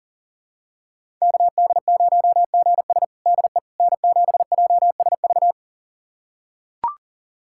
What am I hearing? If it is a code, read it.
Morse code, heard as a keyed tone, about 40 wpm, 700 Hz. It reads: KD0GS DE N7JSV
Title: USA Callsign DE Callsign - 5 Characters - 40wpm